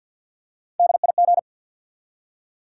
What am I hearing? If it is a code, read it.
Morse code, heard as a keyed tone, about 53 wpm, 700 Hz. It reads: DIG
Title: Three Letter Words - 50wpm